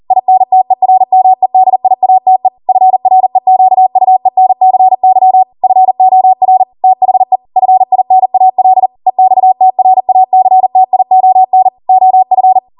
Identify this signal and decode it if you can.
Morse code, heard as a keyed tone, about 40 wpm, 750 Hz. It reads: INTERMEDIATE FREQUENCY FOR THE FINAL EXTRACTION OF